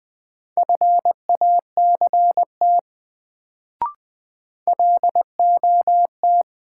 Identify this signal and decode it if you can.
Morse code, heard as a keyed tone, about 20 wpm, 700 Hz. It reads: FACT LOT
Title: Sets of 2 Words - New Word Drill - 20wpm